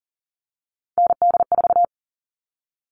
Morse code, heard as a keyed tone, about 40 wpm, 700 Hz. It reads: ND4
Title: All Cut Numbers - 3-Digit 40wpm